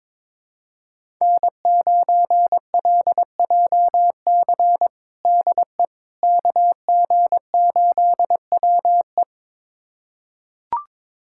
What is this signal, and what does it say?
Morse code, heard as a keyed tone, about 22 wpm, 700 Hz. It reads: N9LJC DE KG8WE
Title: USA Callsign DE Callsign - 5 Characters - 22wpm